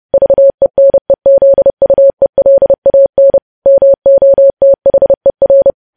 Morse code, heard as a keyed tone, about 30 wpm, 550 Hz. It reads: VENEZUELAN MOTHER
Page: news headlines in morse code at 30 wpm